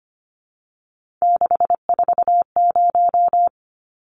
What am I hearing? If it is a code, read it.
Morse code, heard as a keyed tone, about 25 wpm, 690 Hz. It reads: 640
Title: Numbers - 3-Digit 25wpm